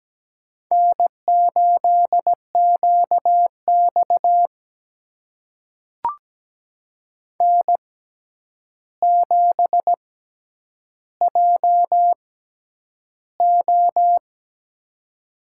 Morse code, heard as a keyed tone, about 17 wpm, spaced as for 9 wpm, 700 Hz. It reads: N8QX N7JO
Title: USA Call Signs - 4 Characters - 6x Character Spacing - 17wpm